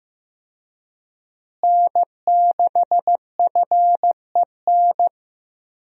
Morse code, heard as a keyed tone, about 15 wpm, 700 Hz. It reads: N6FEN
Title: USA Call Signs - 5 Characters - Repeated 3x - 15wpm